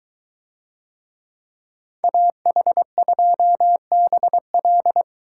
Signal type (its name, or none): Morse code